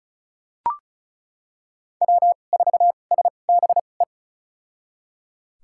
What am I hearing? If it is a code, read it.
Morse code, heard as a keyed tone, about 34 wpm, spaced as for 25 wpm, 700 Hz. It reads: W4SBE
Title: USA Call Signs - 5 Characters - 2x Character Spacing - 35wpm